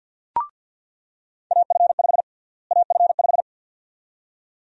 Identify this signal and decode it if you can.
Morse code, heard as a keyed tone, about 47 wpm, 700 Hz. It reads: AF5 AF5
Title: USA Call Sign Prefixes - Repeated 2x - 50wpm